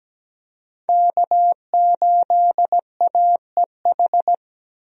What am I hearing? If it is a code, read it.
Morse code, heard as a keyed tone, about 17 wpm, 700 Hz. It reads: K8AEH
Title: USA Call Signs - 5 Characters - 17wpm